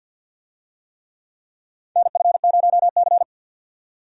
Morse code, heard as a keyed tone, about 52 wpm, 700 Hz. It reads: NF0C
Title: USA Call Signs - 4 Characters - 50wpm